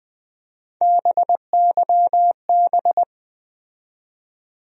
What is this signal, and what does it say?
Morse code, heard as a keyed tone, about 20 wpm, 700 Hz. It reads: BYB
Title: USA Call Sign Postfixes - 3 Letters 20wpm